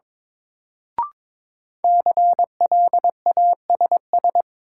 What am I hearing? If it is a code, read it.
Morse code, heard as a keyed tone, about 22 wpm, 700 Hz. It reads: CLASS
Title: Sentences from Top 1000 Words - Review Part 2 - New Word Drill - 22wpm